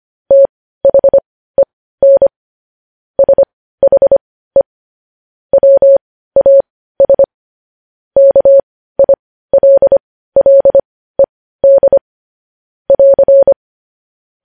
Morse code, heard as a keyed tone, about 25 wpm, spaced as for 15 wpm, 550 Hz. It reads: THEN SHE WAS KILLED <AR>
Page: news-headlines-in-morse-code-at-15-wpm